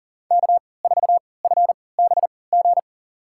Morse code, heard as a keyed tone, about 40 wpm, spaced as for 23 wpm, 700 Hz. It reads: K4FBG
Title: USA Call Signs - 5 Characters - 3x Character Spacing - 40wpm